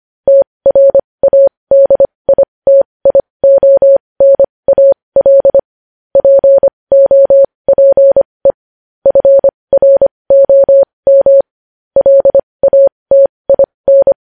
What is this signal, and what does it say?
Morse code, heard as a keyed tone, about 25 wpm, spaced as for 20 wpm, 550 Hz. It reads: TRADITIONAL POPE FROM LATIN
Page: News Headlines in Morse Code at 20 WPM